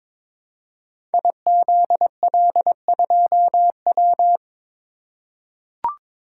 Morse code, heard as a keyed tone, about 22 wpm, 700 Hz. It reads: IZL2W
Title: Code Groups - 5 Letter:Numbers - 22wpm